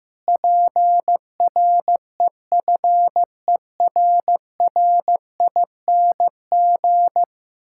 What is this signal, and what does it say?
Morse code, heard as a keyed tone, about 15 wpm, 700 Hz. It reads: PREFERRING
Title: Pangram Challenge v2 - 15wpm